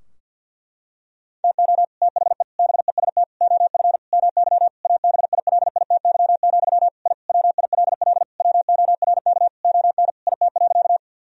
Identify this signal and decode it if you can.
Morse code, heard as a keyed tone, about 46 wpm, 700 Hz. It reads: TO THE BEST OF MY ABILITY, I WILL WORK ON IT.
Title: Sentences from Top 1000 Words - Review Part 3 - 50wpm